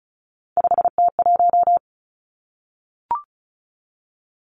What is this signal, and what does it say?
Morse code, heard as a keyed tone, about 36 wpm, 700 Hz. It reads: 5T1
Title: Three Challenging Letter-Number Combinations 35wpm